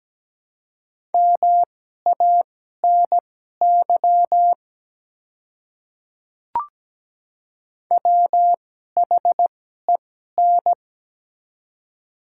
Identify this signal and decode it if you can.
Morse code, heard as a keyed tone, about 17 wpm, spaced as for 12 wpm, 700 Hz. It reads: MANY WHEN
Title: 2x Character Spacing - Top 200 Words - 17wpm